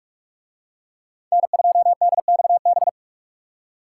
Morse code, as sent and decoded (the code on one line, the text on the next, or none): -. ..--- -.. -..- -...
N2DXB